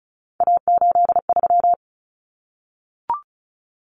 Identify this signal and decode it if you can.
Morse code, heard as a keyed tone, about 34 wpm, 700 Hz. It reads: A83
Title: Common Cut Numbers - 3-Digit 35wpm